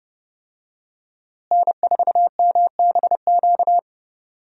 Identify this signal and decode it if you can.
Morse code, heard as a keyed tone, about 30 wpm, 700 Hz. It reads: N4MBQ
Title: USA Call Signs - 5 Characters - Repeated 3x - 30wpm